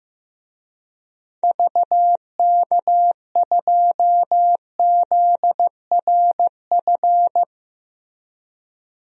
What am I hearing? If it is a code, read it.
Morse code, heard as a keyed tone, about 15 wpm, 700 Hz. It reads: VK2ZRF
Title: DXCC-100 Call Signs - 6 Characters - Repeated 2x - 15wpm